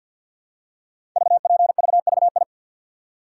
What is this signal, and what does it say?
Morse code, heard as a keyed tone, about 52 wpm, 700 Hz. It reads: VPVVI
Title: Code Groups - 5 Letters - 50wpm